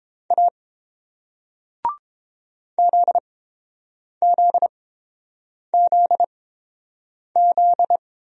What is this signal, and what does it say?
Morse code, heard as a keyed tone, about 30 wpm, spaced as for 7 wpm, 690 Hz. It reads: A ZZZZ